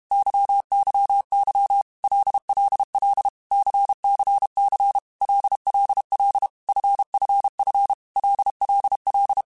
Morse code, heard as a keyed tone, about 32 wpm, 800 Hz. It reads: YYY LLL CCC LLL FFF LLL
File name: Groups@30 / Groups@30 mp3s / LFCYQ@30